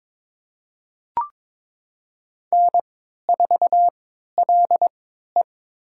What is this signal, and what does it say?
Morse code, heard as a keyed tone, about 22 wpm, spaced as for 12 wpm, 700 Hz. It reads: N4LE
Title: USA Call Signs - 4 Characters - 3x Character Spacing - 22wpm